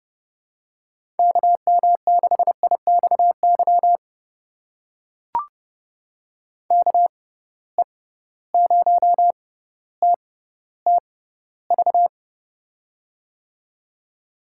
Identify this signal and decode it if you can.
Morse code, heard as a keyed tone, about 32 wpm, spaced as for 16 wpm, 700 Hz. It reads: KM6IXY KE0TTV
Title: USA Call Signs - 6 Characters - 6x Character Spacing - 30wpm